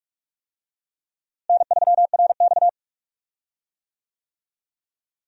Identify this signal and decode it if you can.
Morse code, heard as a keyed tone, about 47 wpm, 700 Hz. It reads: N3RX
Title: USA Call Signs - 4 Characters - Repeated 3x - 45wpm